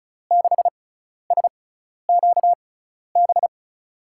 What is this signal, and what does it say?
Morse code, heard as a keyed tone, about 36 wpm, spaced as for 12 wpm, 700 Hz. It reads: 6SQB
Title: USA Call Signs - 6 Characters - 6x Character Spacing - 35wpm